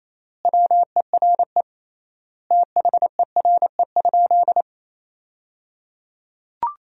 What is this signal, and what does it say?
Morse code, heard as a keyed tone, about 28 wpm, 700 Hz. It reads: WERE THERE?
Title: 3x Word-Spacing - Sentences from Top 500 Words - 28wpm